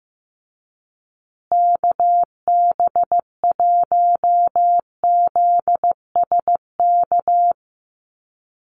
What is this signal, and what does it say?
Morse code, heard as a keyed tone, about 15 wpm, 700 Hz. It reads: KB1ZSK
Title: USA Call Signs Repeated 2x 15wpm